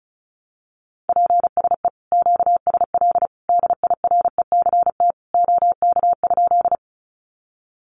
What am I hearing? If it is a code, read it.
Morse code, heard as a keyed tone, about 34 wpm, 700 Hz. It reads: PSE QSL DIRECT OK?
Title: Common QSO Phrases 35wpm